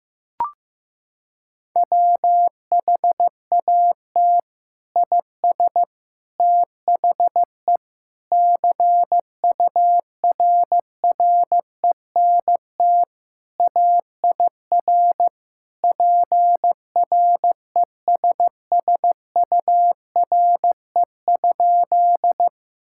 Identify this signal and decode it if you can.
Morse code, heard as a keyed tone, about 15 wpm, 700 Hz. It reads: WHAT IS THE CURRENT AIR PRESSURE?
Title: Sentences from Top 1000 Words - Review Part 6 - 15wpm